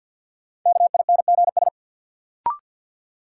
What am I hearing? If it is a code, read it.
Morse code, heard as a keyed tone, about 46 wpm, 700 Hz. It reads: KINGS
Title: Sentences from Top 1000 Words - Review Part 6 - New Word Drill - 50wpm